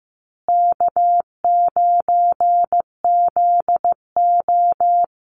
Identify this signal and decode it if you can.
Morse code, heard as a keyed tone, about 15 wpm, 700 Hz. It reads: K9ZO
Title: Easy Call Signs Repeated 3x - 15wpm